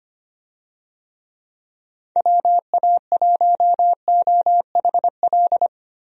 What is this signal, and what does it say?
Morse code, heard as a keyed tone, about 25 wpm, 700 Hz. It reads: WA1OHL
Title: USA Call Signs - 6 Characters - Repeated 2x - 25wpm